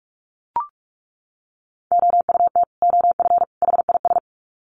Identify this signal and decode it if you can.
Morse code, heard as a keyed tone, about 44 wpm, 700 Hz. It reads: OUT OF HIS
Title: Sets of 3 Words 45wpm